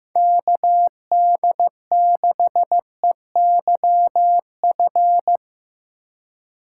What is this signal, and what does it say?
Morse code, heard as a keyed tone, about 15 wpm, 700 Hz. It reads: KD6EYF
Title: USA Call Signs - 6 Characters - 4x Character Spacing - 15wpm